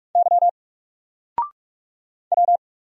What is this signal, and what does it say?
Morse code, heard as a keyed tone, about 44 wpm, 700 Hz. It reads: Y W
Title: Single Letter-Number Alt 45wpm